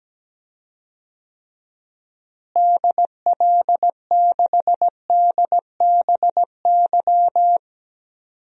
Morse code, heard as a keyed tone, about 17 wpm, 700 Hz. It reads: DL6DBY
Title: DXCC-100 Call Signs - 6 Characters - Repeated 3x - 17wpm